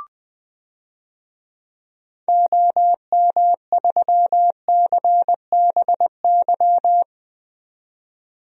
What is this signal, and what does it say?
Morse code, heard as a keyed tone, about 20 wpm, 700 Hz. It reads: OM3CBY
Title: DXCC-100 Call Signs - 6 Characters - 20wpm